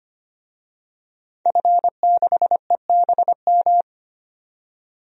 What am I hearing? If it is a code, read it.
Morse code, heard as a keyed tone, about 25 wpm, 700 Hz. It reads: F6EBM
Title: DXCC-100 Call Signs - 5 Characters - 25wpm